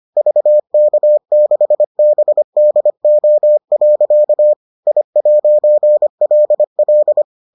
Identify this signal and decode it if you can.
Morse code, heard as a keyed tone, about 25 wpm, 600 Hz. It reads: VK6BDO. I'LL